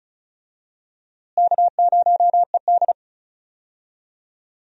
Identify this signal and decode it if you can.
Morse code, heard as a keyed tone, about 36 wpm, 700 Hz. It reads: K0ED
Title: USA Call Signs - 4 Characters - Repeated 3x - 35wpm